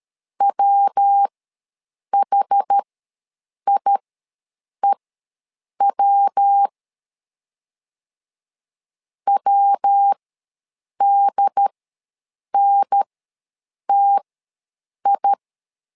Morse code, heard as a keyed tone, about 13 wpm, spaced as for 7 wpm, 800 Hz. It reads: WHIEW WDNTI
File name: LEZIONI / 3 MWRO / 3 MW ws 12wpm 1816